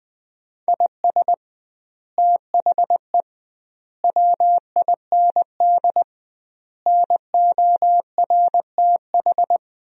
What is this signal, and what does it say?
Morse code, heard as a keyed tone, about 20 wpm, 700 Hz. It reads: IS THE WIND NORTH
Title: 2x Word-Spacing - Sentences from Top 400 Words - 20wpm